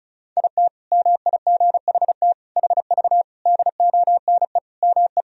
Morse code, heard as a keyed tone, about 36 wpm, 700 Hz. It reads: IT MIGHT HV DONE ME
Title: Sentences from Top 500 Words and Abbreviations 35wpm